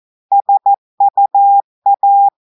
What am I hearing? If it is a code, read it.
Morse code, heard as a keyed tone, about 14 wpm, 800 Hz. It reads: SUA
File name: audios / portugais / portugais texte en clair 1 vitesse 14 mots minutes